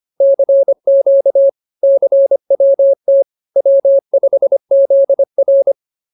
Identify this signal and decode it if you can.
Morse code, heard as a keyed tone, about 25 wpm, 550 Hz. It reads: CQ CWT W5ZR
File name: CWT-209-25